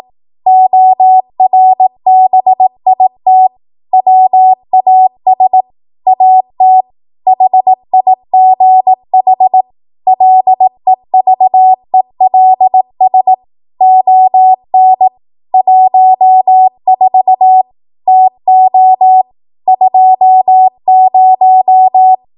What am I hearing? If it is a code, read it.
Morse code, heard as a keyed tone, about 18 wpm, 750 Hz. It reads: ORBIT WAS AT HIGH LEVELS ON 14 TO 20
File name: Morse / 18WPM